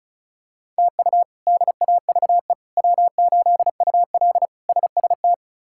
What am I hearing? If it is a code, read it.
Morse code, heard as a keyed tone, about 34 wpm, 700 Hz. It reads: TU DAVE W8UL SST
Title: K1USN SST QSO Practice - 35wpm